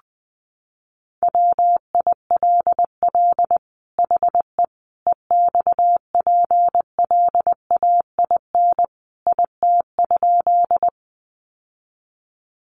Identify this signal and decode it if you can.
Morse code, heard as a keyed tone, about 20 wpm, 700 Hz. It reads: WILL HE EXPLAIN IT?
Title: Sentences from Top 1100 Words 20wpm